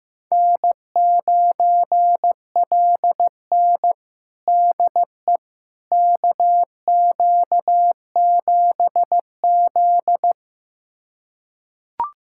Morse code, heard as a keyed tone, about 15 wpm, 700 Hz. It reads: N9LN DE KQ7Z
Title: USA Callsign DE Callsign - 4 Characters - 15wpm